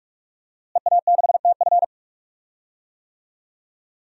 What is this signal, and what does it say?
Morse code, heard as a keyed tone, about 46 wpm, 700 Hz. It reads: EA6TF
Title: DXCC-100 Call Signs - 5 Characters - Repeated 3x - 45wpm